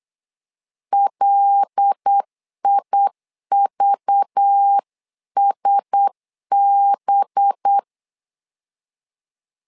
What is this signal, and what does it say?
Morse code, heard as a keyed tone, about 8 wpm, 800 Hz. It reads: LIVSB
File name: LEZIONI / 5 LBVC / 5 LBV 8wpm 2837